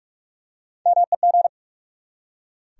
Morse code, heard as a keyed tone, about 43 wpm, 700 Hz. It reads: MEG
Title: USA Common Names - Max 3 Letters - 45wpm